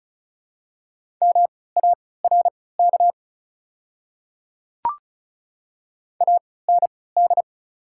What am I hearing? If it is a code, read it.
Morse code, heard as a keyed tone, about 36 wpm, spaced as for 20 wpm, 700 Hz. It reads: MARK AND
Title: 3x Character Spacing - Top 500 Words - 35wpm